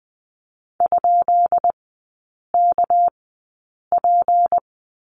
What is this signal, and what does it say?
Morse code, heard as a keyed tone, about 20 wpm, spaced as for 8 wpm, 700 Hz. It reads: ?KP
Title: 198 Lesson 32e - Letter k - Three Characters 20wpm